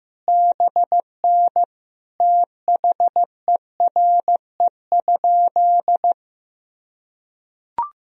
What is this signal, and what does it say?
Morse code, heard as a keyed tone, about 15 wpm, 700 Hz. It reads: BN THERE?
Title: Sentences from Top 100 Words and Abbreviations 15wpm